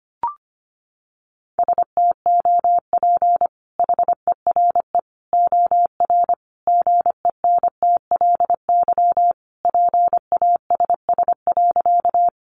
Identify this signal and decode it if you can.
Morse code, heard as a keyed tone, about 25 wpm, 700 Hz. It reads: STOP HERE OR GENTLY PASS.
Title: Sentences from Top 1500 Words 25wpm